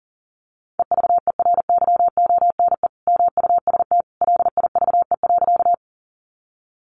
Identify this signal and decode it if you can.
Morse code, heard as a keyed tone, about 40 wpm, 700 Hz. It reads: EVERYONE MUST LIVE.
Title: Sentences from Top 800 Words 40wpm